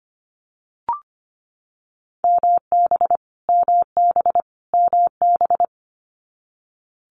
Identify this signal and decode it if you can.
Morse code, heard as a keyed tone, about 25 wpm, 700 Hz. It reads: MB MB MB
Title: Multi-letter Phonemes Repeated 3x 25wpm